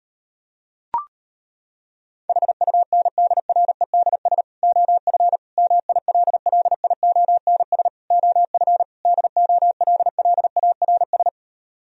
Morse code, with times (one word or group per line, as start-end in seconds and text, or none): 2.3-4.4 HUNDREDS
4.6-5.4 OF
5.6-7.9 MILLIONS
8.1-8.8 OF
9.1-11.3 DOLLARS